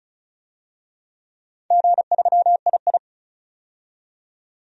Morse code, heard as a keyed tone, about 36 wpm, 700 Hz. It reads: G3II
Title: DXCC-100 Call Signs - 4 Characters - 35wpm